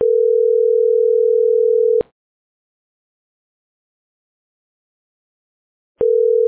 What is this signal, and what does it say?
Ringback tone (North America), 440 + 480 Hz.